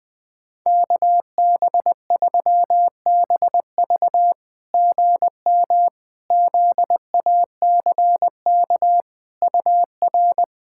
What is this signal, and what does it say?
Morse code, heard as a keyed tone, about 20 wpm, 700 Hz. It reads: KB3BV GM ZACK UR
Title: SOTA QSO Practice 20wpm